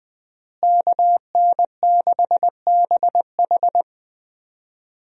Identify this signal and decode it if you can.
Morse code, heard as a keyed tone, about 20 wpm, 700 Hz. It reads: KN6BH
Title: USA Call Signs - 5 Characters - Repeated 2x - 20wpm